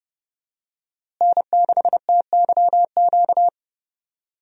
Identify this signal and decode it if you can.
Morse code, heard as a keyed tone, about 31 wpm, 700 Hz. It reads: N6TYQ
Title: USA Call Signs - 5 Characters - Repeated 2x - 30wpm